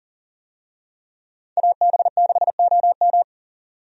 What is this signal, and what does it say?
Morse code, heard as a keyed tone, about 37 wpm, 700 Hz. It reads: AB6OM
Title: USA Call Signs - 5 Characters - 40wpm